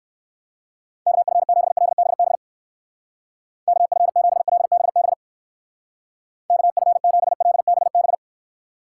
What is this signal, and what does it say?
Morse code, heard as a keyed tone, about 57 wpm, 700 Hz. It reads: KF7LBB KF7LBB KF7LBB